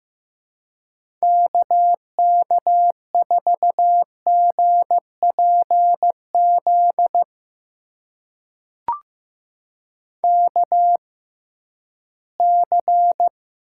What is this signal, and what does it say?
Morse code, heard as a keyed tone, about 15 wpm, spaced as for 11 wpm, 700 Hz. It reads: KK4GPZ KC
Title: USA Call Signs - 6 Characters - 6x Character Spacing - 15wpm